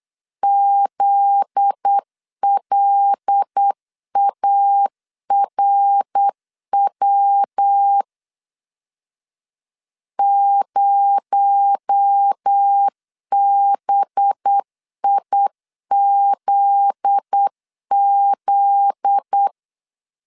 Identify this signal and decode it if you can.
Morse code, heard as a keyed tone, about 8 wpm, 800 Hz. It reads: ZLARW 0BIZZ
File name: LEZIONI / 7 GF9Z / 7 GF9Z 8wpm 3213